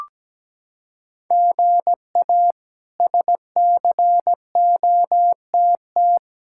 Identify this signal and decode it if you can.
Morse code, heard as a keyed tone, about 17 wpm, 700 Hz. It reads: GA SCOTT
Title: K1USN SST QSO Practice - 17wpm